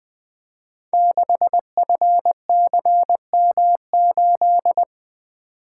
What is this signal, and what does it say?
Morse code, heard as a keyed tone, about 21 wpm, 700 Hz. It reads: 6FCM8
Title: Code Groups - 5 Letter:Numbers - 20wpm